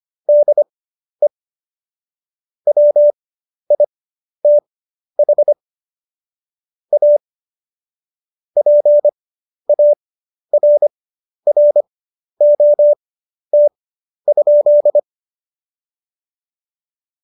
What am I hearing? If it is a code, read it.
Morse code, heard as a keyed tone, about 25 wpm, spaced as for 11 wpm, 590 Hz. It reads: DE WITH A PARROT?
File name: Jokes-25-char-10-eff